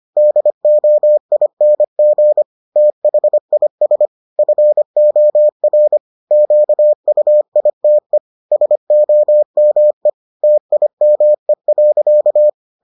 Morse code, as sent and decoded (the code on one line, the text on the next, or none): -.. --- .. -. --. / - .... .. ... / ..-. --- .-. / --.- ..- .. - . / ... --- -- . / - .. -- . .-.-.-
DOING THIS FOR QUITE SOME TIME.